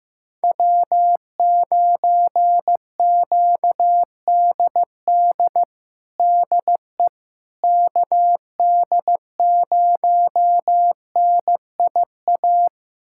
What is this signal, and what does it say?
Morse code, heard as a keyed tone, about 15 wpm, 700 Hz. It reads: W9QDD DE KD0NIA
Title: USA Callsign DE Callsign 15wpm